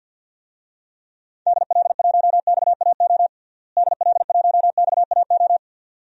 Morse code, heard as a keyed tone, about 46 wpm, 700 Hz. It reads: DL1XAO DL1XAO
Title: DXCC-100 Call Signs - 6 Characters - Repeated 2x - 50wpm